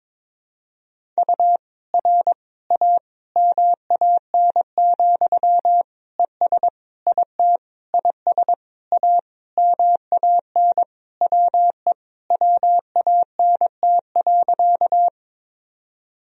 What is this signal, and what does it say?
Morse code, heard as a keyed tone, about 21 wpm, 700 Hz. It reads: U R A MAN, ES IT IS A MAN WE WANT.
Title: Sentences from Top 200 Words and Abbreviations 22wpm